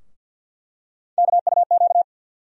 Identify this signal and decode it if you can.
Morse code, heard as a keyed tone, about 51 wpm, 700 Hz. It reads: KUQ
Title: USA Call Sign Postfixes - 3 Letters 50wpm